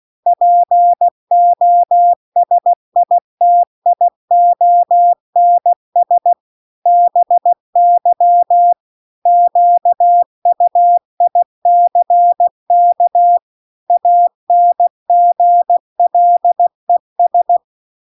Morse code, heard as a keyed tone, about 16 wpm, 700 Hz. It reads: POSITIONS BY QUICK ANGLES
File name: CW-The-Big-Twoheaded-River-1-16WPM